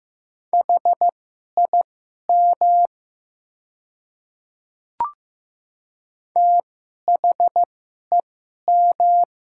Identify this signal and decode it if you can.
Morse code, heard as a keyed tone, about 15 wpm, spaced as for 11 wpm, 700 Hz. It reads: HIM THEM
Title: 2x Character Spacing - Top 200 Words - 15wpm